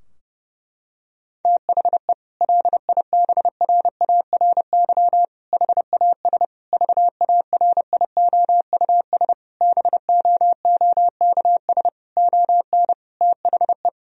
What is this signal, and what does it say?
Morse code, heard as a keyed tone, about 30 wpm, 700 Hz. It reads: THE LIBRARY HAS VARIOUS BOOKS ON THE
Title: Sentences from Top 1000 Words - Review Part 4 - 30wpm